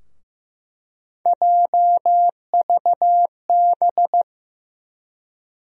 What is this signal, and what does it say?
Morse code, heard as a keyed tone, about 15 wpm, 700 Hz. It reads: JVB
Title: USA Call Sign Postfixes - 3 Letters 15wpm